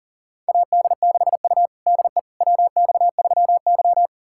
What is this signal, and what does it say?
Morse code, heard as a keyed tone, about 40 wpm, 700 Hz. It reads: AD6U DE WX3Y
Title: USA Callsign DE Callsign - 4 Characters - 40wpm